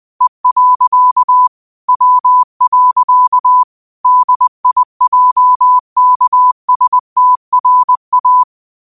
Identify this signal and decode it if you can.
Morse code, heard as a keyed tone, about 20 wpm, 1000 Hz. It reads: E. W. DIJKSTRA